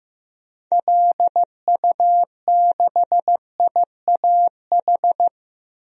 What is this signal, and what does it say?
Morse code, heard as a keyed tone, about 15 wpm, 700 Hz. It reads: LU6IAH
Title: DXCC-100 Call Signs - 6 Characters - Repeated 2x - 15wpm